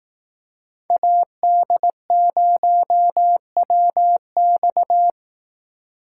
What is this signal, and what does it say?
Morse code, heard as a keyed tone, about 19 wpm, 700 Hz. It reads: AD0WX
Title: Speed Racing - USA 2x2 Call Signs 15wpm